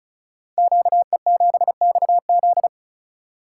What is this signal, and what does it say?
Morse code, heard as a keyed tone, about 36 wpm, 700 Hz. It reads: QE7XZ
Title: Code Groups - 5 Letter:Numbers - 35wpm